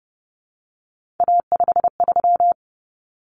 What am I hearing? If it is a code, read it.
Morse code, heard as a keyed tone, about 30 wpm, 700 Hz. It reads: A53
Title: Common Cut Numbers - 3-Digit 30wpm